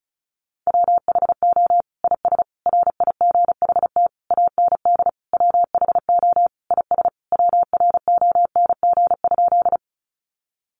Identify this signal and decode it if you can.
Morse code, heard as a keyed tone, about 34 wpm, 700 Hz. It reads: WHO IS RIGHT AND WHO IS WRONG?
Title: Sentences from Top 900 Words 35wpm